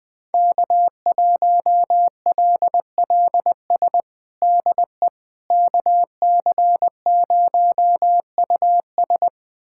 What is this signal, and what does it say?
Morse code, heard as a keyed tone, about 20 wpm, 700 Hz. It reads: K1LLS DE KC0US
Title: USA Callsign DE Callsign - 5 Characters - 20wpm